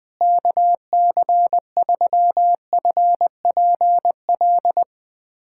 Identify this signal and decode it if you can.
Morse code, heard as a keyed tone, about 20 wpm, 700 Hz. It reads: KC3FPL
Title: USA Callsign DE Callsign - 6 Characters - 20wpm